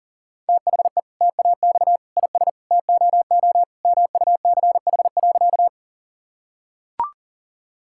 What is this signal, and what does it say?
Morse code, heard as a keyed tone, about 40 wpm, 700 Hz. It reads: THE TAX IS TOO MUCH.
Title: Sentences from Top 1000 Words - Review Part 5 - 40wpm